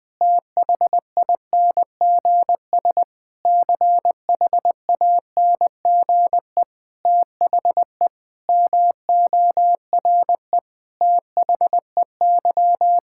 Morse code, heard as a keyed tone, about 20 wpm, 700 Hz. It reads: THINGS CHANGE THE MORE THEY
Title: Sentences with 7-Word Phrases - Spoken Beginning - 20wpm